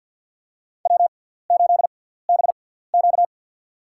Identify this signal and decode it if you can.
Morse code, heard as a keyed tone, about 54 wpm, spaced as for 17 wpm, 700 Hz. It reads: W8BQ
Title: USA Call Signs - 4 Characters - 6x Character Spacing - 50wpm